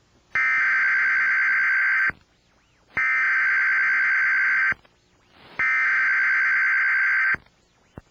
Emergency Alert System (SAME) tones (header bursts), heard as harsh buzzing bursts. It is heard from 0.4 to 7.3 s.